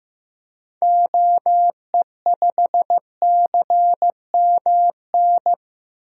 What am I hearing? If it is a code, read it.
Morse code, heard as a keyed tone, about 16 wpm, 700 Hz. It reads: OE5CMN
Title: DXCC-100 Call Signs - 6 Characters - Repeated 2x - 15wpm